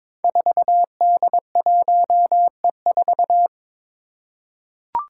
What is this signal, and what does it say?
Morse code, heard as a keyed tone, about 22 wpm, 700 Hz. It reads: 4D1E4
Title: Five Challenging Letter-Number Combinations 22wpm